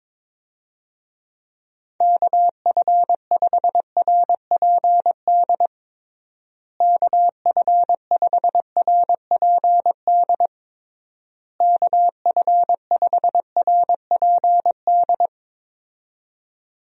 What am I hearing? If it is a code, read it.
Morse code, heard as a keyed tone, about 21 wpm, 700 Hz. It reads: KF5RPD KF5RPD KF5RPD
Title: USA Call Signs - 6 Characters - Repeated 3x - 22wpm